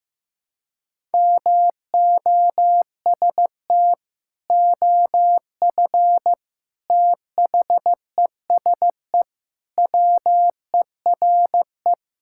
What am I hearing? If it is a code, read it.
Morse code, heard as a keyed tone, about 15 wpm, 700 Hz. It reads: MOST OF THESE WERE
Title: Sentences from Top 1000 Words - Review Part 5 - 15wpm